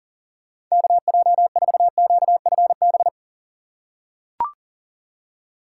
Morse code, heard as a keyed tone, about 40 wpm, 700 Hz. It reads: KJ4QFB
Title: USA Call Signs - 6x Character Spacing - 40wpm